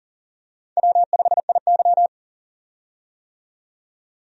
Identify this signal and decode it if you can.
Morse code, heard as a keyed tone, about 40 wpm, 700 Hz. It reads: W5IY
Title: USA Call Signs - 4 Characters - 40wpm